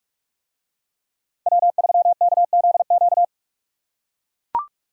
Morse code, heard as a keyed tone, about 44 wpm, 700 Hz. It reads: W3KZQ